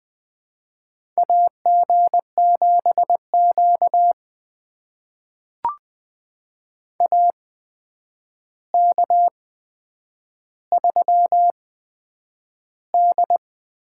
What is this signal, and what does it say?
Morse code, heard as a keyed tone, about 20 wpm, spaced as for 9 wpm, 700 Hz. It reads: AG7Q AK3D
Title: USA Call Signs - 4 Characters - 8x Character Spacing - 20wpm